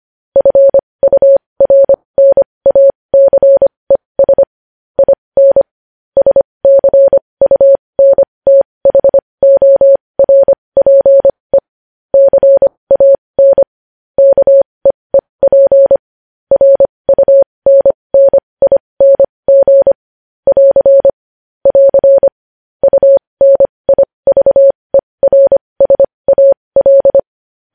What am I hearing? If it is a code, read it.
Morse code, heard as a keyed tone, about 25 wpm, spaced as for 20 wpm, 550 Hz. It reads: FURNACES IN SCUNTHORPE CAN KEEP RUNNING <AR> <AR> UNIVERSAL
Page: news-headlines-in-morse-code-at-20-wpm